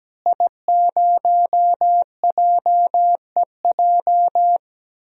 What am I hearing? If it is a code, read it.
Morse code, heard as a keyed tone, about 17 wpm, 700 Hz. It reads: I0JEJ
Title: DXCC-100 Call Signs - 5 Characters - Repeated 2x - 17wpm